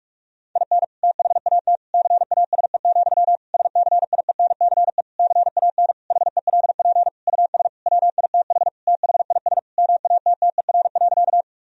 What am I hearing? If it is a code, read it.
Morse code, heard as a keyed tone, about 46 wpm, 700 Hz. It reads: IN THAT CASE, SCIENCE CAN HELP US WITH THIS MATTER.
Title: Sentences from Top 1000 Words - Review Part 2 - 45wpm